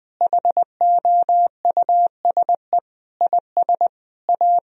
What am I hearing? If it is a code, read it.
Morse code, heard as a keyed tone, about 20 wpm, 700 Hz. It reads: HOUSE IS A
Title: Sentences from Top 1000 Words - Review Part 4 - 20wpm